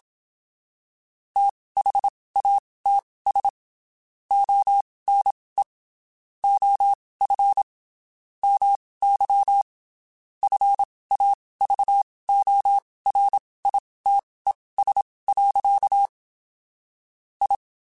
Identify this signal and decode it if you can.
Morse code, heard as a keyed tone, about 27 wpm, spaced as for 19 wpm, 800 Hz. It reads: THATS ONE OF MY FAVORITES. I
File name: (spaced) / QSOs Spaced-mp3s / WD0FBT 1 @25 sp